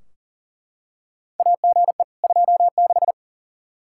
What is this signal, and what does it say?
Morse code, heard as a keyed tone, about 40 wpm, 700 Hz. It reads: AGE 26
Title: Practice QSO - Exchange 3 - More Info - 40wpm